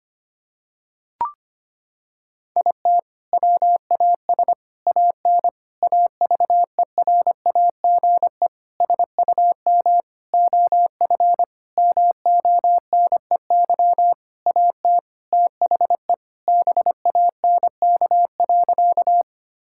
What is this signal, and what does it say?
Morse code, heard as a keyed tone, about 25 wpm, 700 Hz. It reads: IT WAS AN AVERAGE SUM OF MONEY AT THE BANK.
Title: Sentences from Top 1000 Words - Review Part 6 - 25wpm